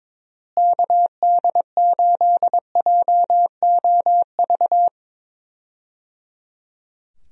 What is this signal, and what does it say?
Morse code, heard as a keyed tone, about 22 wpm, 700 Hz. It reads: KD8JOV
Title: USA Call Signs - 6 Characters - Repeated 3x - 22wpm